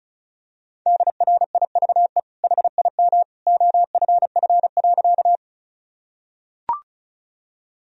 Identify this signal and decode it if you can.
Morse code, heard as a keyed tone, about 36 wpm, 700 Hz. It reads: DRIVE HIM OFF.
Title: Sentences from Top 1000 Words - Review Part 2 - 35wpm